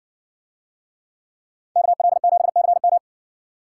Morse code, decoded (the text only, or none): KL7CM